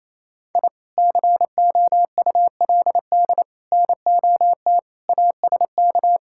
Morse code, heard as a keyed tone, about 28 wpm, 700 Hz. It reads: I COULD NOT ASK
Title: Sentences from Top 1000 Words - Review Part 5 - 28wpm